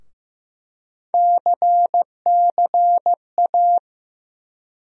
Morse code, heard as a keyed tone, about 15 wpm, 700 Hz. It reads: CCA
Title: USA Call Sign Postfixes - 3 Letters 15wpm